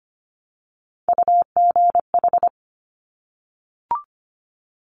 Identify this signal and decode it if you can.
Morse code, heard as a keyed tone, about 25 wpm, 700 Hz. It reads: UGH
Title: Top Three Letter Combinations in Words 25wpm